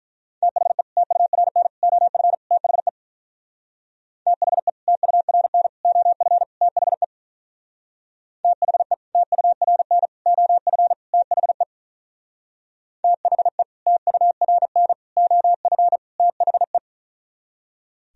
Morse code, heard as a keyed tone, about 43 wpm, 700 Hz. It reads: THE TURN OF THE THE TURN OF THE THE TURN OF THE THE TURN OF THE
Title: Speed-Racing - Sets of 4 Words 35wpm